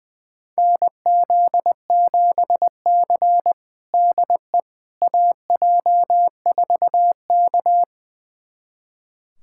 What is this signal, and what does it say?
Morse code, heard as a keyed tone, about 21 wpm, 700 Hz. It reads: NZ7C DE AJ4K
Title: USA Callsign DE Callsign - 4 Characters - 20wpm